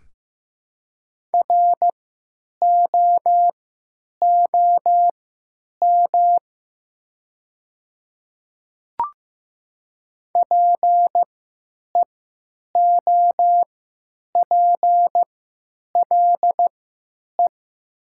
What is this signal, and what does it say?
Morse code, heard as a keyed tone, about 15 wpm, spaced as for 9 wpm, 700 Hz. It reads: ROOM PEOPLE
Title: 3x Character Spacing - Top 500 Words - 15wpm